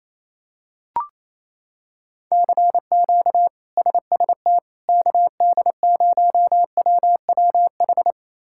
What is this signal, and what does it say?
Morse code, heard as a keyed tone, about 28 wpm, 700 Hz. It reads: CQ SST KD0WWH
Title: K1USN SST QSO Practice - 28wpm